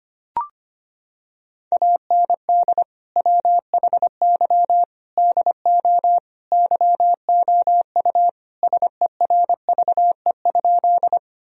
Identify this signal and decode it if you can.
Morse code, heard as a keyed tone, about 25 wpm, 700 Hz. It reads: AND WHY DO YOU SERVE?
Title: Sentences from Top 1000 Words - Review Part 2 - 25wpm